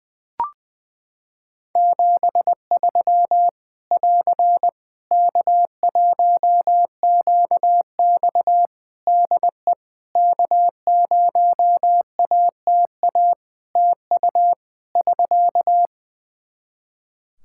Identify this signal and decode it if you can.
Morse code, heard as a keyed tone, about 20 wpm, 700 Hz. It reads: 73 <AR> K1QX DE K0ATA TU <SK>
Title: Practice QSO - Exchange 4 - Ending - 20wpm